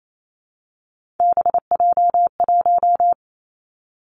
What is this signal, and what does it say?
Morse code, heard as a keyed tone, about 28 wpm, 700 Hz. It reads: BJ1
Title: Three Challenging Letter-Number Combinations 28wpm